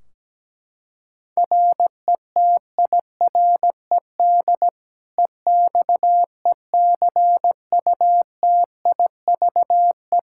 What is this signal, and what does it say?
Morse code, heard as a keyed tone, about 17 wpm, 700 Hz. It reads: RETIRED EXECUTIVE
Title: Practice QSO - Exchange 3 - More Info - 17wpm